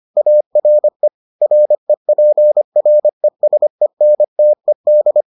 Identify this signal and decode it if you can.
Morse code, heard as a keyed tone, about 25 wpm, 600 Hz. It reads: ARE REPRESENTED